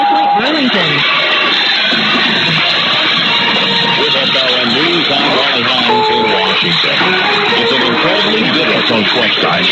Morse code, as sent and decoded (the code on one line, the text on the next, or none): none